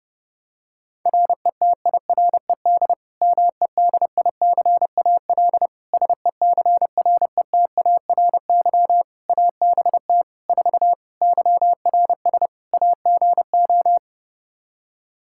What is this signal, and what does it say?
Morse code, heard as a keyed tone, about 29 wpm, 700 Hz. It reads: RETIRED MEDICAL SECRETARY ABT 4 YRS AGO
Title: Practice QSO - Exchange 3 - More Info - 30wpm